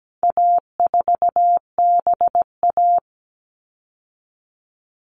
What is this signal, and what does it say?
Morse code, heard as a keyed tone, about 17 wpm, 700 Hz. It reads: A4BA
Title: All Cut Numbers - 4-Digit 17wpm